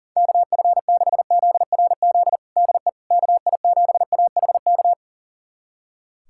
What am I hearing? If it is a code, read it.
Morse code, heard as a keyed tone, about 40 wpm, 700 Hz. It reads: KF6ZRZ DE KI7AHK
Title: USA Callsign DE Callsign 40wpm